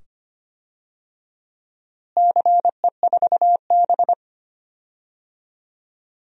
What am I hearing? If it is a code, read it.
Morse code, heard as a keyed tone, about 25 wpm, 700 Hz. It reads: CE4B
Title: DXCC-100 Call Signs - 4 Characters - Repeated 2x - 25wpm